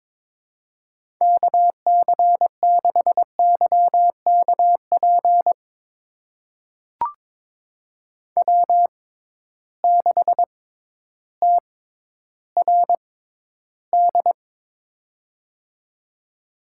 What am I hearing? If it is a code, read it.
Morse code, heard as a keyed tone, about 22 wpm, spaced as for 12 wpm, 700 Hz. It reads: KC6YKP W6TRD
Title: USA Call Signs - 6x Character Spacing - 22wpm